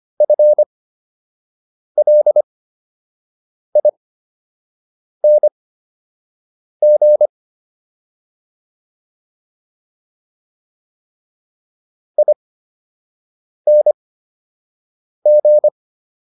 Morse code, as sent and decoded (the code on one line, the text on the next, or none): ..-. .-.. .. -. --. / .. -. --.
FLING ING